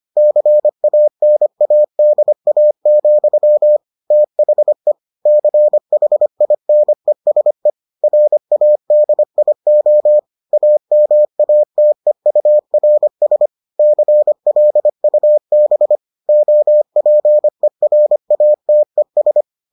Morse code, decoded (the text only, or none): CANADA, THE CHINESE RADIO AMATEURS CLUB OPERATES